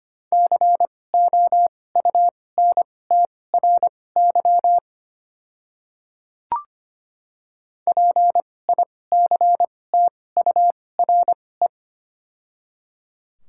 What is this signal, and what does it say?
Morse code, heard as a keyed tone, about 25 wpm, spaced as for 18 wpm, 700 Hz. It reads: COUNTRY PICTURE
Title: 2x Character Spacing - Top 300 Words - 25wpm